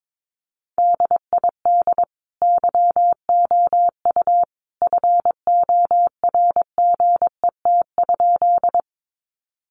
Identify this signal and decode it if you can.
Morse code, heard as a keyed tone, about 22 wpm, 700 Hz. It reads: DID YOU FORGET?
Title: Sentences from Top 1200 Words 22wpm